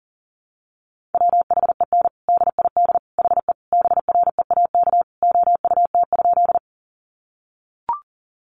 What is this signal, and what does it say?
Morse code, heard as a keyed tone, about 40 wpm, 700 Hz. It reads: WHEN DID HE BREAK OUT?
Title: Sentences from Top 900 Words 40wpm